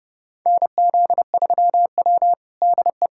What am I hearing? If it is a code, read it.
Morse code, heard as a keyed tone, about 30 wpm, 700 Hz. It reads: NZ3W DE
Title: USA Callsign DE Callsign - 4 Characters - 30wpm